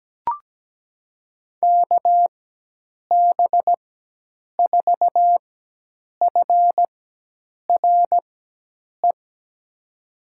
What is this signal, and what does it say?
Morse code, heard as a keyed tone, about 17 wpm, spaced as for 8 wpm, 700 Hz. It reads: KB4FRE
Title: USA Call Signs - 6 Characters - 4x Character Spacing - 17wpm